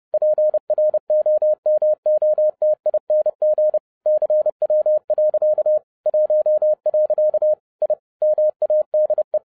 Morse code, heard as a keyed tone, about 30 wpm, 610 Hz. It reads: PROMOTING CW. 1. I MADE